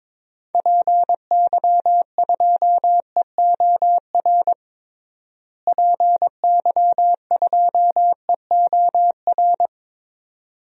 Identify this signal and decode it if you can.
Morse code, heard as a keyed tone, about 22 wpm, 700 Hz. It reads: PY2EOR PY2EOR
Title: DXCC-100 Call Signs - 6 Characters - Repeated 2x - 22wpm